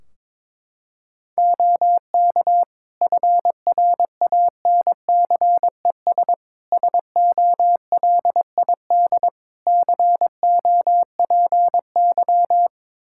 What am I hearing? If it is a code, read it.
Morse code, heard as a keyed tone, about 22 wpm, 700 Hz. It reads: OK FRANCES SOLID COPY
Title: Practice QSO - Exchange 3 - More Info - 22wpm